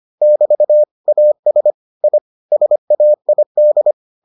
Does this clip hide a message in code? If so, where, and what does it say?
Yes: Morse code from 0.2 to 3.9 s, heard as a keyed tone, about 25 wpm, 600 Hz. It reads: <BT> AS I SAID